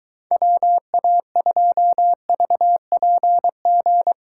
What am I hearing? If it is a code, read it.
Morse code, heard as a keyed tone, about 23 wpm, 700 Hz. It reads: WA2VPG